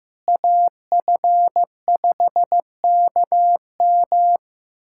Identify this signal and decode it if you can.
Morse code, heard as a keyed tone, about 15 wpm, 700 Hz. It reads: AF5KM
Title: USA Callsign DE Callsign 15wpm